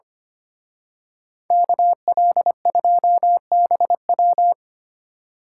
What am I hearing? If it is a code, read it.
Morse code, heard as a keyed tone, about 30 wpm, 700 Hz. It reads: KL2BW
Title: Speed Racing - USA 2x2 Call Signs 25wpm